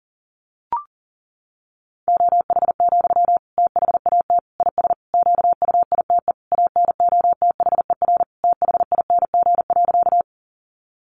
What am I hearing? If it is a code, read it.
Morse code, heard as a keyed tone, about 40 wpm, 700 Hz. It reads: OH, THAT IS QUITE ANOTHER THING.
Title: Sentences from Top 700 Words 40wpm